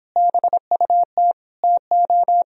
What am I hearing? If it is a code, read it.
Morse code, heard as a keyed tone, about 26 wpm, 700 Hz. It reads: BUT TO